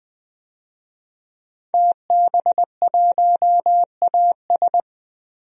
Morse code, heard as a keyed tone, about 21 wpm, 700 Hz. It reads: TB1AS
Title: DXCC-100 Call Signs - 5 Characters - Repeated 2x - 20wpm